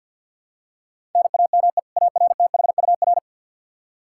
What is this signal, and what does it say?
Morse code, heard as a keyed tone, about 51 wpm, 700 Hz. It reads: NAME ARTHUR
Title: Practice QSO - Exchange 1 - Beginning - 50wpm